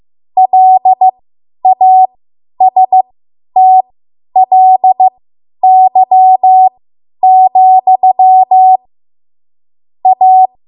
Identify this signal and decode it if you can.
Morse code, heard as a keyed tone, about 15 wpm, spaced as for 10 wpm, 750 Hz. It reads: LASTLY, A
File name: Morse / 10WPM